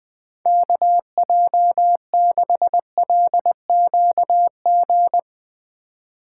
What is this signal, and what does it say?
Morse code, heard as a keyed tone, about 21 wpm, 700 Hz. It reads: KJ6LQG
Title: USA Call Signs - 6 Characters - 3x Character Spacing - 20wpm